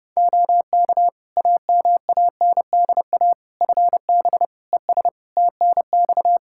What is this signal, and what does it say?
Morse code, heard as a keyed tone, about 30 wpm, 700 Hz. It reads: OK AMANDA FB ES TNX
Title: K1USN SST QSO Practice 30wpm